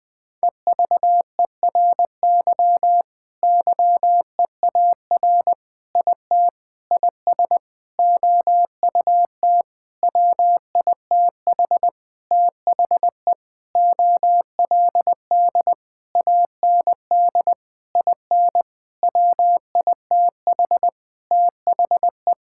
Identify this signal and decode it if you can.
Morse code, heard as a keyed tone, about 20 wpm, 700 Hz. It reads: EVERY YEAR IT IS OUT WITH THE OLD AND IN WITH THE
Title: Sentences with 9-Word Phrases - 20wpm